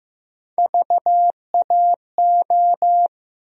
Morse code, heard as a keyed tone, about 15 wpm, 700 Hz. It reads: VAO
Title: USA Call Sign Postfixes - 3 Letters 15wpm